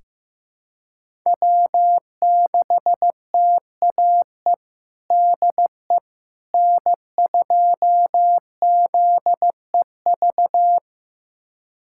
Morse code, heard as a keyed tone, about 15 wpm, 700 Hz. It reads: W6TAE DE N2ZEV
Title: USA Callsign DE Callsign - 5 Characters - 15wpm